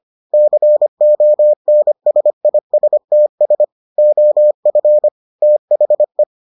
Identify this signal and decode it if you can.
Morse code, heard as a keyed tone, about 25 wpm, 600 Hz. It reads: CONSISTS OF THE